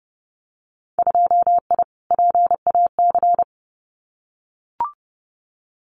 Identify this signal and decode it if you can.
Morse code, heard as a keyed tone, about 30 wpm, 700 Hz. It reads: 2I PAC
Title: Winter Field Day Exchanges Encore 30wpm